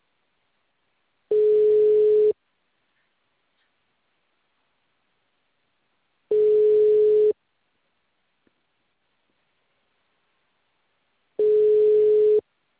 Ringback tone (Europe, 425 Hz).